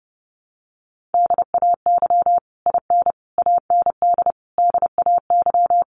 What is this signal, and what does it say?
Morse code, heard as a keyed tone, about 30 wpm, 700 Hz. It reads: DAY IN AND DAY
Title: Sets of 5 Words Encore2 30wpm